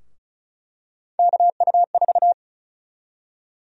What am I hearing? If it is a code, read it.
Morse code, heard as a keyed tone, about 36 wpm, 700 Hz. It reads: KU4
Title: USA Call Sign Prefixes - 35wpm